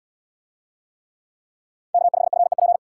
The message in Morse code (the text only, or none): K5FEP